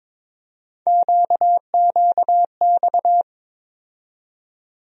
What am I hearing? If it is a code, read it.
Morse code, heard as a keyed tone, about 22 wpm, 700 Hz. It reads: QQX
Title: USA Call Sign Postfixes - 3 Letters 22wpm